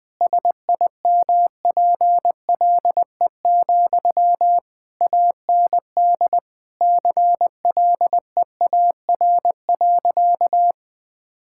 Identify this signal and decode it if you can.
Morse code, heard as a keyed tone, about 20 wpm, 700 Hz. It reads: SIMPLE, AND CLEAR.
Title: Sentences from Top 1000 Words - Review Part 6 - 20wpm